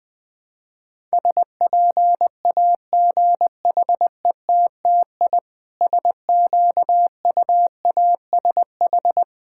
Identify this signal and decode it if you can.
Morse code, heard as a keyed tone, about 20 wpm, 700 Hz. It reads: SPAGHETTI SQUASH
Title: Difficult Words with Hints - 20wpm